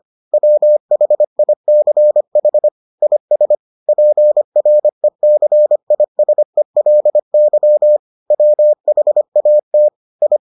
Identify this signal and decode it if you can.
Morse code, heard as a keyed tone, about 25 wpm, 600 Hz. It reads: WHICH IS PRECISELY WHAT I